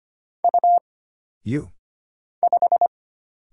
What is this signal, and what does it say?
Morse code, heard as a keyed tone, about 25 wpm, 700 Hz. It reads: U 5
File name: Single Character (Letter:Number:Punctuation:Prosign) - Mind-Melt - 25wpm